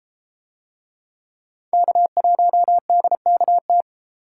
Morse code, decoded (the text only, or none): K1DKT